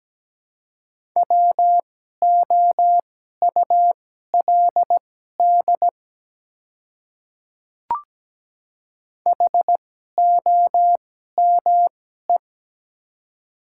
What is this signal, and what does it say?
Morse code, heard as a keyed tone, about 17 wpm, spaced as for 12 wpm, 700 Hz. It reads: WOULD HOME
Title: 2x Character Spacing - Top 200 Words - 17wpm